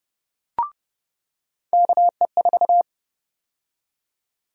Morse code, heard as a keyed tone, about 30 wpm, 700 Hz. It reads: KE4
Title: USA Call Sign Prefixes - 30wpm